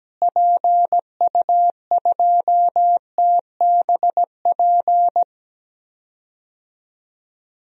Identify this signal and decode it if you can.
Morse code, heard as a keyed tone, about 17 wpm, 700 Hz. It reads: PU2TBP
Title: DXCC-100 Call Signs - 6 Characters - Repeated 2x - 17wpm